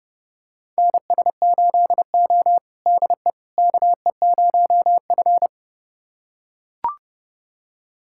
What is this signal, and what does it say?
Morse code, heard as a keyed tone, about 30 wpm, 700 Hz. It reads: NS8O DE KE0F
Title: USA Callsign DE Callsign - 4 Characters - 30wpm